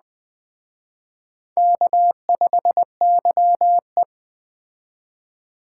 Morse code, heard as a keyed tone, about 19 wpm, 700 Hz. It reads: K5YE